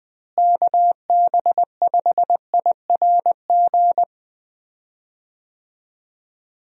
Morse code, heard as a keyed tone, about 20 wpm, 700 Hz. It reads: KB5IRG
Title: USA Call Signs - 6 Characters - Repeated 3x - 20wpm